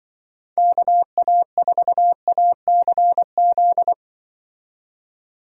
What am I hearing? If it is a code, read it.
Morse code, heard as a keyed tone, about 25 wpm, 700 Hz. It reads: KA4ACZ
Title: Speed Racing - USA 2x3 Call Signs 20wpm